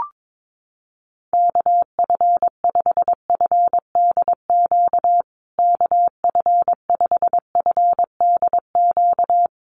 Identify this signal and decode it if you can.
Morse code, heard as a keyed tone, about 22 wpm, 700 Hz. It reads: KF5FDQ KF5FDQ
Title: USA Call Signs Repeated 2x 22wpm